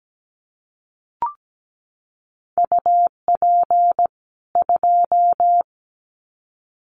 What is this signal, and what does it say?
Morse code, heard as a keyed tone, about 17 wpm, 700 Hz. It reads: UP 2
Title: Common QSO Phrases 17wpm